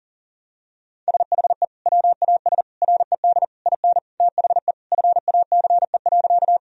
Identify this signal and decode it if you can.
Morse code, heard as a keyed tone, about 37 wpm, 700 Hz. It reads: SHE WAS RED IN THE FACE.
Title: Sentences from Top 1000 Words - Review Part 2 - 40wpm